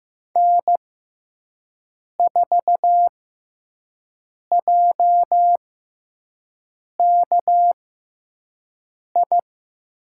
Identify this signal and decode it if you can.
Morse code, heard as a keyed tone, about 15 wpm, spaced as for 5 wpm, 700 Hz. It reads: N4JKI
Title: USA Call Signs - 6x Character Spacing - 15wpm